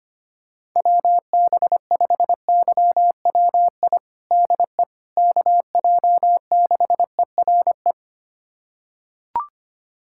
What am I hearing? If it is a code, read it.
Morse code, heard as a keyed tone, about 25 wpm, 700 Hz. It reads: WB5YWI DE KJ6ERE
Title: USA Callsign DE Callsign - 6 Characters - 25wpm